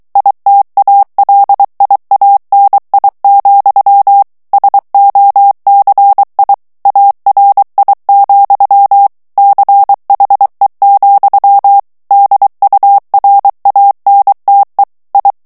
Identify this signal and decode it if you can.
Morse code, heard as a keyed tone, about 23 wpm, 800 Hz. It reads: ITALIANI, SOCI ARI, CHE, DURANTE I
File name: ITALIEN texte clair 2 vitesse 22 mots minute